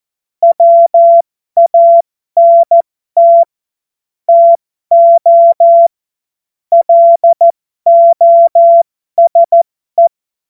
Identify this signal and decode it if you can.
Morse code, heard as a keyed tone, about 14 wpm, spaced as for 12 wpm, 680 Hz. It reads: WANT TO LOSE
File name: Morsekurs / Witze